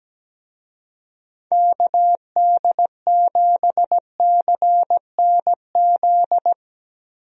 Morse code, heard as a keyed tone, about 17 wpm, 700 Hz. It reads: KD7CNZ
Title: USA Call Signs - 6 Characters - Repeated 2x - 17wpm